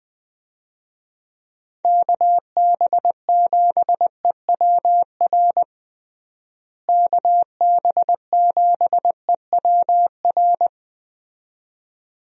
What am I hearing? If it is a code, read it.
Morse code, heard as a keyed tone, about 21 wpm, 700 Hz. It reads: KB7EWR KB7EWR
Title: USA Call Signs - 6 Characters - Repeated 2x - 20wpm